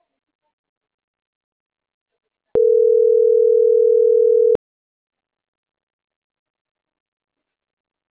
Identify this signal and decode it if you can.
Ringback tone (North America), 440 + 480 Hz.